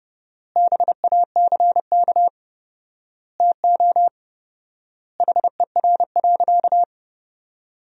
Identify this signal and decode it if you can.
Morse code, heard as a keyed tone, about 31 wpm, 700 Hz. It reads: BACK TO HER.
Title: 4x Word-Spacing - Sentences from Top 300 Words - 30wpm